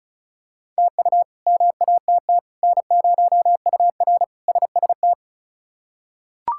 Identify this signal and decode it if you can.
Morse code, heard as a keyed tone, about 36 wpm, 700 Hz. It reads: TU MATT N0UR SST
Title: K1USN SST QSO Practice - 35wpm